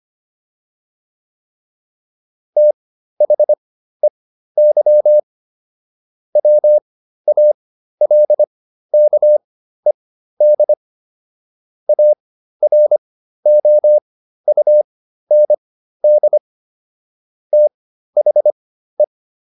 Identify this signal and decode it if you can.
Morse code, heard as a keyed tone, about 25 wpm, spaced as for 13 wpm, 600 Hz. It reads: THEY WALKED AROUND THE